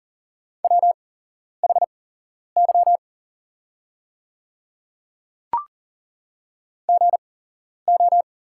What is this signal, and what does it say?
Morse code, heard as a keyed tone, about 38 wpm, spaced as for 11 wpm, 700 Hz. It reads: WHY GO
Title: 8x Character Spacing - Top 200 Words - 40wpm